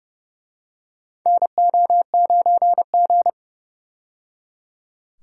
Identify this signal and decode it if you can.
Morse code, heard as a keyed tone, about 30 wpm, 700 Hz. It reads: NO9G
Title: USA Call Signs - 4 Characters - 30wpm